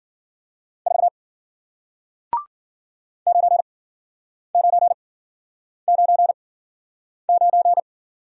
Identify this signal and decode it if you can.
Morse code, heard as a keyed tone, about 48 wpm, spaced as for 9 wpm, 700 Hz. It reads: 4 9999
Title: Speed-Racing Single Letter-Number 40wpm